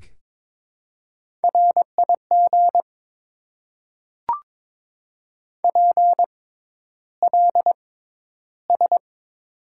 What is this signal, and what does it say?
Morse code, heard as a keyed tone, about 22 wpm, spaced as for 11 wpm, 700 Hz. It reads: RIG PLS
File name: Top 90 QSO Elements - 6x Character Spacing - 22wpm